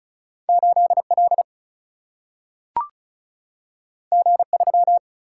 Morse code, heard as a keyed tone, about 36 wpm, 700 Hz. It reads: 8 L G 3
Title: Single-Letter Single-Number 35wpm